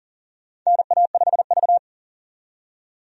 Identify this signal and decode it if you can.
Morse code, heard as a keyed tone, about 39 wpm, 700 Hz. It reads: NA5V
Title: USA Call Signs - 4 Characters - 40wpm